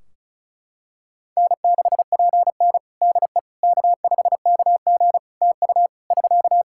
Morse code, heard as a keyed tone, about 36 wpm, 700 Hz. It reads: N6PN DE K5KG TU <SK>
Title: Practice QSO - Exchange 4 - Ending - 35wpm